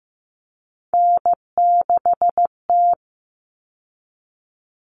Morse code, heard as a keyed tone, about 15 wpm, 700 Hz. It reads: N6T
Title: All Cut Numbers - 3-Digit 15wpm